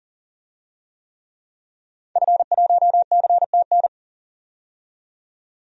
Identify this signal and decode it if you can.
Morse code, heard as a keyed tone, about 40 wpm, 700 Hz. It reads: F1CTN
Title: DXCC-100 Call Signs - 5 Characters - 40wpm